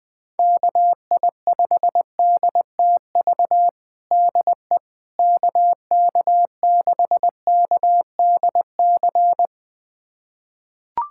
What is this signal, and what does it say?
Morse code, heard as a keyed tone, about 20 wpm, 700 Hz. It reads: KI5DTV DE KK6KDC
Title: USA Callsign DE Callsign - 6 Characters - 20wpm